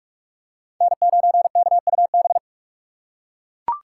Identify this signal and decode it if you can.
Morse code, heard as a keyed tone, about 42 wpm, 700 Hz. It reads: N9KUB